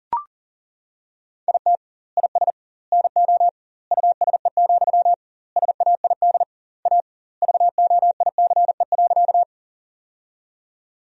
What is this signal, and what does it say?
Morse code, heard as a keyed tone, about 40 wpm, 700 Hz. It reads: IT IS NO USE, SAID A VOICE.
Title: 2x Word-Spacing - Sentences from Top 500 Words - 40wpm